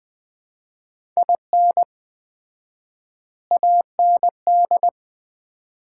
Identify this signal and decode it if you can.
Morse code, heard as a keyed tone, about 20 wpm, 700 Hz. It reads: IN AND
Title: 4x Word-Spacing - Sets of 4 Words - 20wpm